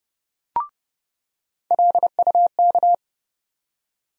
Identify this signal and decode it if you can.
Morse code, heard as a keyed tone, about 30 wpm, 700 Hz. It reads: LUK
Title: USA Call Sign Postfixes - 3 Letters 30wpm